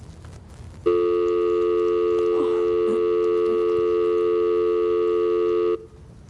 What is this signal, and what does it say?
Dial tone (North America), 350 + 440 Hz.